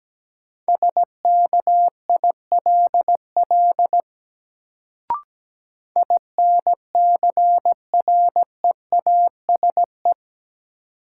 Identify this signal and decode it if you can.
Morse code, heard as a keyed tone, about 17 wpm, 700 Hz. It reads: SKILL INCREASE
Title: Sentences from Top 1200 Words - New Word Drill Alt - 17wpm